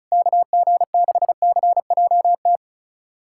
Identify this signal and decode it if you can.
Morse code, heard as a keyed tone, about 36 wpm, 700 Hz. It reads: KG6CJT
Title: USA Callsign DE Callsign 35wpm